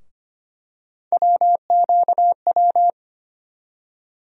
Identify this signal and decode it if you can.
Morse code, heard as a keyed tone, about 25 wpm, 700 Hz. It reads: WQW
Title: USA Call Sign Postfixes - 3 Letters 25wpm